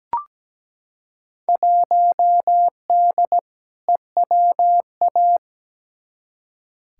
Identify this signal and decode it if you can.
Morse code, heard as a keyed tone, about 17 wpm, 700 Hz. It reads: 1D EWA
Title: Field Day Exchanges - 17wpm